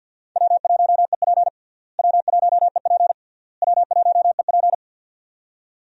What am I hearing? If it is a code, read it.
Morse code, heard as a keyed tone, about 50 wpm, 700 Hz. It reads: W1EP W1EP W1EP
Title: USA Call Signs - 4 Characters - Repeated 3x - 50wpm